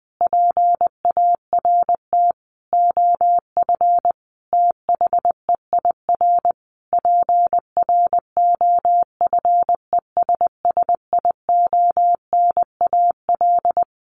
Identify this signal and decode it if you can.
Morse code, heard as a keyed tone, about 20 wpm, 700 Hz. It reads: PART OF THEIR PROFESSIONAL
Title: Sentences from Top 1400 Words 20wpm